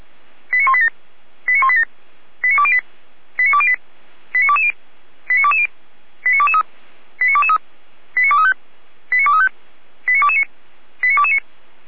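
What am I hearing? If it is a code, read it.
Five-tone selective call (ZVEI), heard as quick runs of five tones, tones of about 70 ms. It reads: 89187 89187 89208 89208 89220 89220 89223 89223 89235 89235 89229 89229
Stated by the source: Recorded via SDR by feuerwehr-orlamuende.org (fire brigade Orlamünde), ZVEI five-tone call 89240.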